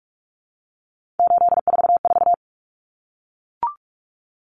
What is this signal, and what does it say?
Morse code, heard as a keyed tone, about 44 wpm, 700 Hz. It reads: 844
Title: Numbers - 3-Digit 45wpm